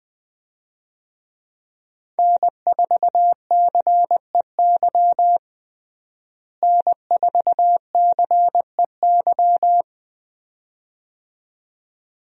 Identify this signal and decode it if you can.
Morse code, heard as a keyed tone, about 20 wpm, 700 Hz. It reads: N4CEY N4CEY
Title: USA Call Signs - 5 Characters - Repeated 2x - 20wpm